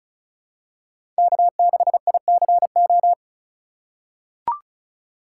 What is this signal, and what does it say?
Morse code, heard as a keyed tone, about 34 wpm, 700 Hz. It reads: K6ICO
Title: USA Call Signs - 5 Characters - 3x Character Spacing - 35wpm